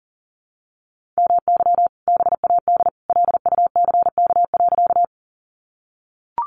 Morse code, heard as a keyed tone, about 40 wpm, 700 Hz. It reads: MY BAD LUCK.
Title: Sentences from Top 1500 Words 40wpm